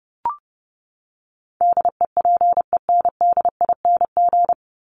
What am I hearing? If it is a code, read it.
Morse code, heard as a keyed tone, about 29 wpm, 700 Hz. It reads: DEPENDING
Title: ING Suffix - Top 100 Words with Base Words 30wpm